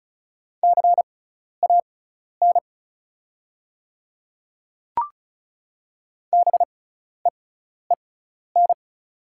Morse code, heard as a keyed tone, about 36 wpm, spaced as for 12 wpm, 690 Hz. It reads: CAN BEEN
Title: 6x Character Spacing - Top 100 Words - 35wpm